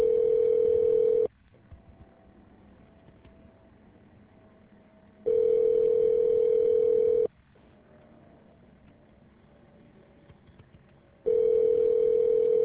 Ringback tone (North America), 440 + 480 Hz.